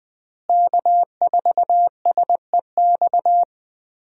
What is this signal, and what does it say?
Morse code, heard as a keyed tone, about 21 wpm, 700 Hz. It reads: K4SEX
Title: USA Call Signs - 5 Characters - Repeated 3x - 20wpm